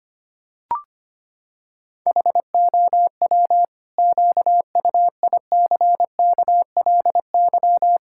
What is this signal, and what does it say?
Morse code, heard as a keyed tone, about 25 wpm, 700 Hz. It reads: HOW QUICKLY
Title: Pangram Challenge v2 - 25wpm